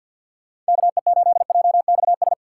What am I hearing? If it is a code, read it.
Morse code, heard as a keyed tone, about 52 wpm, 700 Hz. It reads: KE8JXS
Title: USA Call Signs - 6 Characters - 4x Character Spacing - 50wpm